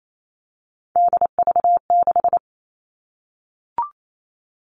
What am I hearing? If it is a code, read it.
Morse code, heard as a keyed tone, about 28 wpm, 700 Hz. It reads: DV6
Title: All Cut Numbers - 3-Digit 28wpm